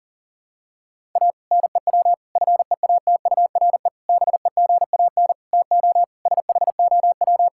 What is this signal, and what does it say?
Morse code, heard as a keyed tone, about 40 wpm, 700 Hz. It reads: A NEW FEATURE BEGAN TO SHOW
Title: Sentences from Top 1000 Words - Review Part 5 - 40wpm